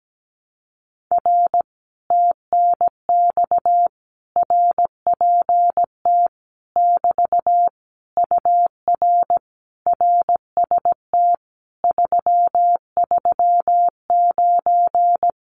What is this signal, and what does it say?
Morse code, heard as a keyed tone, about 17 wpm, 700 Hz. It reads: R TNX RPT <BT> UR RST 339
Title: Common QSO Phrases Encore 17wpm